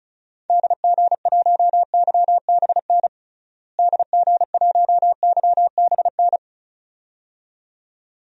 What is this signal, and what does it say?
Morse code, heard as a keyed tone, about 36 wpm, 700 Hz. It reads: DG1YBN DG1YBN
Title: DXCC-100 Call Signs - 6 Characters - Repeated 2x - 35wpm